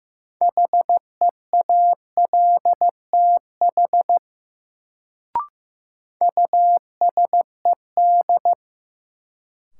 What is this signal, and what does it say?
Morse code, heard as a keyed tone, about 15 wpm, 700 Hz. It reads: HEALTH USED
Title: Sentences from Top 600 Words - New Word Drill - 15wpm